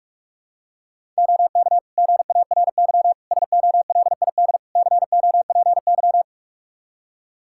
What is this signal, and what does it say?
Morse code, heard as a keyed tone, about 44 wpm, 700 Hz. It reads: OK GARY SOLID COPY
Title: Practice QSO - Exchange 3 - More Info - 45wpm